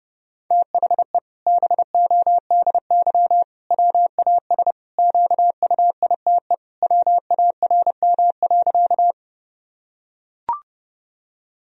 Morse code, heard as a keyed tone, about 32 wpm, 700 Hz. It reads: THE BODY WAS QUITE WARM.
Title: Sentences from Top 1000 Words - Review Part 2 - 30wpm